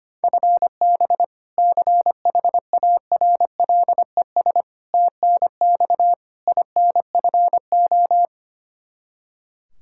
Morse code, heard as a keyed tone, about 25 wpm, 700 Hz. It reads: FB CHARLES TNX INFO
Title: Practice QSO - Exchange 3 - More Info - 25wpm